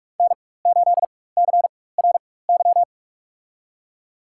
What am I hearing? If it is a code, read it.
Morse code, heard as a keyed tone, about 46 wpm, spaced as for 21 wpm, 700 Hz. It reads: N8CRY
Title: USA Call Signs - 5 Characters - 4x Character Spacing - 45wpm